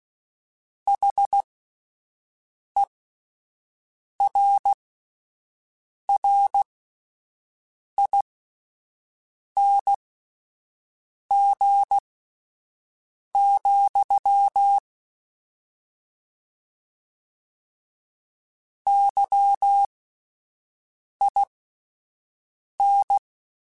Morse code, heard as a keyed tone, about 16 wpm, spaced as for 5 wpm, 800 Hz. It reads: HERRING, YIN
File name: Misc / Misc-mp3s / Word Pairs @5